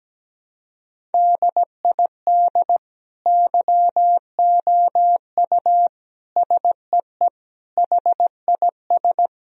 Morse code, heard as a keyed tone, about 17 wpm, 700 Hz. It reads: DID YOU SEE HIS
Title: Sentences from Top 1000 Words - Review Part 1 - 17wpm